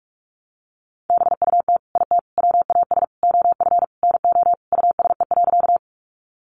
Morse code, heard as a keyed tone, about 44 wpm, 700 Hz. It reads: BUT IT WAS OF NO USE.
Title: Sentences from Top 100 Words 45wpm